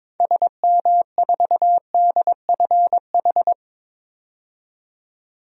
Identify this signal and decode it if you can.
Morse code, heard as a keyed tone, about 21 wpm, 700 Hz. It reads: SM4DFH
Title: DXCC-100 Call Signs - 6 Characters - 22wpm